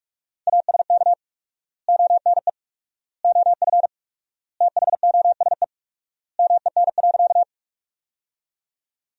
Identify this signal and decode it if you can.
Morse code, heard as a keyed tone, about 46 wpm, 700 Hz. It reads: ASK ONE OF THOSE MEN.
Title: 4x Word-Spacing - Sentences from Top 300 Words - 45wpm